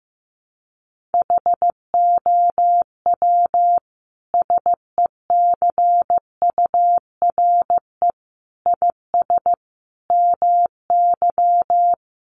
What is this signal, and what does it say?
Morse code, heard as a keyed tone, about 14 wpm, 700 Hz. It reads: HOW SECURE IS MY
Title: Sentences from Top 1200 Words 15wpm